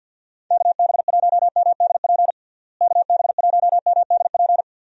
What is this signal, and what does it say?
Morse code, heard as a keyed tone, about 49 wpm, 700 Hz. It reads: KB1MDP KB1MDP
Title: USA Call Signs - 6 Characters - Repeated 2x - 50wpm